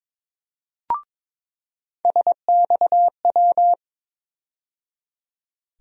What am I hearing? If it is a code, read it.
Morse code, heard as a keyed tone, about 22 wpm, 700 Hz. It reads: SXW
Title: USA Call Sign Postfixes - 3 Letters 22wpm